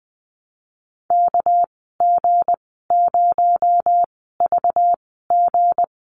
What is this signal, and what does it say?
Morse code, heard as a keyed tone, about 20 wpm, spaced as for 14 wpm, 700 Hz. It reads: KG0VG
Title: 245 Lesson 39g - Letter z - Call Signs 20wpm